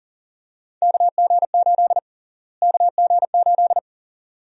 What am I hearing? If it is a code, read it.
Morse code, heard as a keyed tone, about 40 wpm, 700 Hz. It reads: KG8 KG8
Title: USA Call Sign Prefixes - Repeated 2x - 40wpm